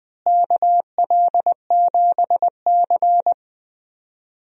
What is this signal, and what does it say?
Morse code, heard as a keyed tone, about 20 wpm, 700 Hz. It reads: KL7C
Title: DXCC-100 Call Signs - 4 Characters - Repeated 2x - 20wpm